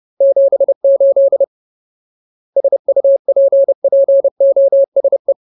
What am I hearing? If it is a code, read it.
Morse code, heard as a keyed tone, about 30 wpm, 550 Hz. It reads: 78 SUPPOSE
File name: Ultimate Word List / 30WPM Ultimate